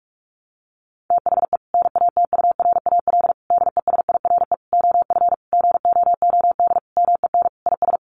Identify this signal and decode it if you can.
Morse code, heard as a keyed tone, about 44 wpm, 700 Hz. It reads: THE NATURAL DESIRE OF GOOD MEN IS
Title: Sentences from Top 1100 Words 45wpm